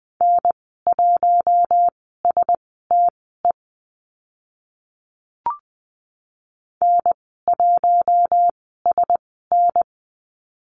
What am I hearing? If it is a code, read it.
Morse code, heard as a keyed tone, about 20 wpm, spaced as for 15 wpm, 700 Hz. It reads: N1STE N1SN
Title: 042 Lesson 8g - Number 1 - Call Signs 20wpm